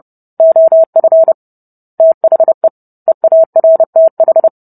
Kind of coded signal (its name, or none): Morse code